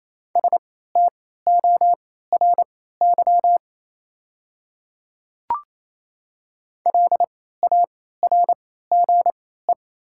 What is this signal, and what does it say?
Morse code, heard as a keyed tone, about 28 wpm, spaced as for 16 wpm, 700 Hz. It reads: STORY LARGE
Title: 3x Character Spacing - Top 500 Words - 28wpm